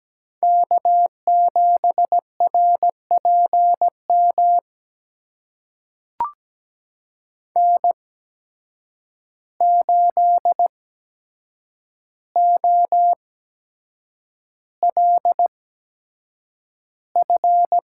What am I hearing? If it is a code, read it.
Morse code, heard as a keyed tone, about 17 wpm, spaced as for 7 wpm, 700 Hz. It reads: K7RPM N8OLF
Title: USA Call Signs - 5 Characters - 8x Character Spacing - 17wpm